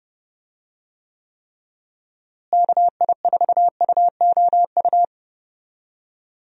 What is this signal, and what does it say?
Morse code, heard as a keyed tone, about 32 wpm, 700 Hz. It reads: KI4UOU
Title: USA Call Signs - 6 Characters - Repeated 3x - 30wpm